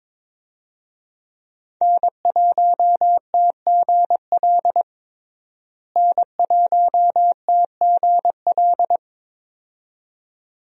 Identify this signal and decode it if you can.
Morse code, heard as a keyed tone, about 22 wpm, 700 Hz. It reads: N1TGL N1TGL
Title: USA Call Signs - 5 Characters - Repeated 2x - 22wpm